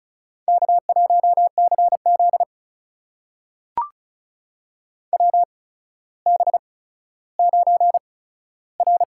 Morse code, heard as a keyed tone, about 34 wpm, spaced as for 15 wpm, 700 Hz. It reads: K1CZ WB9R
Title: USA Call Signs - 4 Characters - 8x Character Spacing - 35wpm